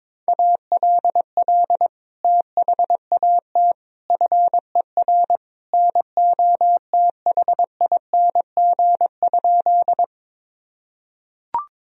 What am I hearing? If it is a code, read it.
Morse code, heard as a keyed tone, about 22 wpm, 700 Hz. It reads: ALL THAT FER NOTHING?
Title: Sentences from Top 500 Words and Abbreviations 22wpm